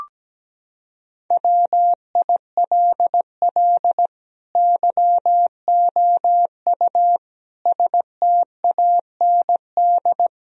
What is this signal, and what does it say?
Morse code, heard as a keyed tone, about 17 wpm, 700 Hz. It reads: WILL YOU STAND
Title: Sentences from Top 1000 Words - Review Part 2 - 17wpm